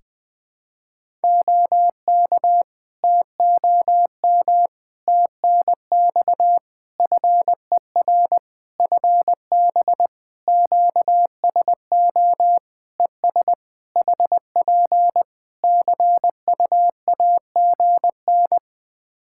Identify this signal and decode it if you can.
Morse code, heard as a keyed tone, about 21 wpm, 700 Hz. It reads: OK TOM TNX FER FB QSO ES HP CUAGN
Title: Practice QSO - Exchange 4 - Ending - 20wpm